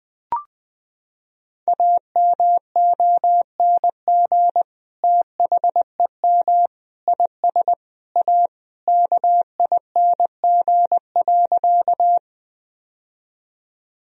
Morse code, heard as a keyed tone, about 21 wpm, 700 Hz. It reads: AMONG THEM IS A KING.
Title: Sentences from Top 1000 Words - Review Part 3 - 20wpm